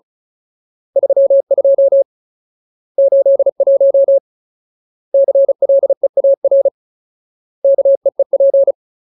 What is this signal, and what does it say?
Morse code, heard as a keyed tone, about 34 wpm, 550 Hz. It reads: 32 81 CLEAR KEEP
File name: Ultimate Word List / 35WPM Ultimate